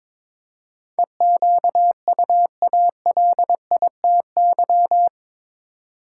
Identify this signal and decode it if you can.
Morse code, heard as a keyed tone, about 22 wpm, 700 Hz. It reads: EQUALITY
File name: practice / 22wpm / Eight Letter Words - 22wpm